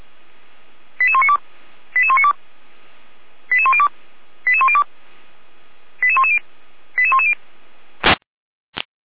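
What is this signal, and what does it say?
Five-tone selective call (ZVEI), heard as quick runs of five tones, tones of about 70 ms. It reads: 88192 88192 88102 88102 88110 88110
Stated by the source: ZVEI five-tone call received by SDR, from the server of the Orlamünde fire brigade (feuerwehr-orlamuende).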